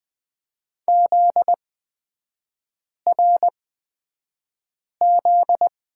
Morse code, heard as a keyed tone, about 20 wpm, spaced as for 5 wpm, 700 Hz. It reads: ZRZ
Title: 243 Lesson 39e - Letter z - Three Characters 20wpm